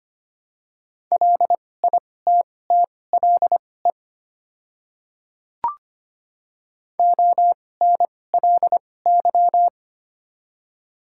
Morse code, heard as a keyed tone, about 25 wpm, spaced as for 18 wpm, 690 Hz. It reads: LITTLE ONLY
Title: 2x Character Spacing - Top 200 Words - 25wpm